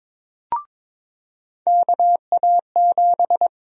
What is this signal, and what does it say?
Morse code, heard as a keyed tone, about 22 wpm, 700 Hz. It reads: KA7
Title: USA Call Sign Prefixes - 22wpm